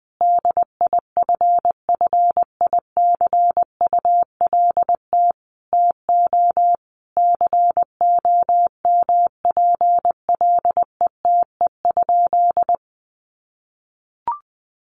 Morse code, heard as a keyed tone, about 20 wpm, 700 Hz. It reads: DIFFICULT TO COMPLETE?
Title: Sentences from Top 1500 Words 20wpm